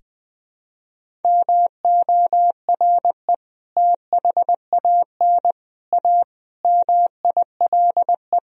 Morse code, heard as a keyed tone, about 20 wpm, 700 Hz. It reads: MORE THAN A MILE
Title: Sentences from Top 1000 Words - Review Part 1 - 20wpm